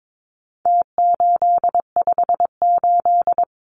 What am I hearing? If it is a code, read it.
Morse code, heard as a keyed tone, about 22 wpm, 700 Hz. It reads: T858
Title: Numbers - Contest Style - 4-Digit 22wpm